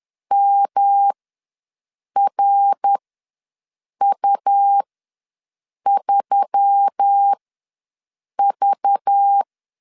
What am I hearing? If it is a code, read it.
Morse code, heard as a keyed tone, about 10 wpm, spaced as for 6 wpm, 800 Hz. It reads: MRU3V
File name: LEZIONI / 5 LBVC / 5 LBV ws 10wpm 3645